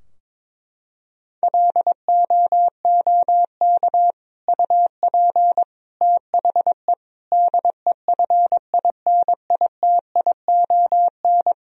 Morse code, heard as a keyed tone, about 22 wpm, 700 Hz. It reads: LOOK UP THE DEFINITION
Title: Sentences from Top 1000 Words - Review Part 4 - 22wpm